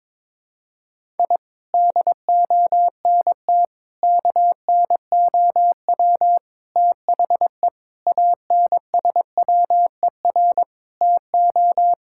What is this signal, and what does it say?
Morse code, heard as a keyed tone, about 22 wpm, 700 Hz. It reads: I DONT KNOW THE ANSWER TO
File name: practice / 22wpm / Sets of 7 Words - 22wpm